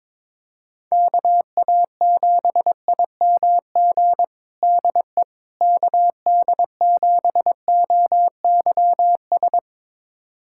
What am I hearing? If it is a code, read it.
Morse code, heard as a keyed tone, about 22 wpm, 700 Hz. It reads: KA7IMG DE KD7OYS
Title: USA Callsign DE Callsign - 6 Characters - 22wpm